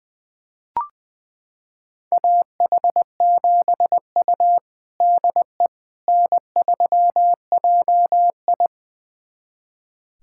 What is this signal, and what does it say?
Morse code, heard as a keyed tone, about 20 wpm, 700 Hz. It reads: AH7U DE N3JI
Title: USA Callsign DE Callsign - 4 Characters - 20wpm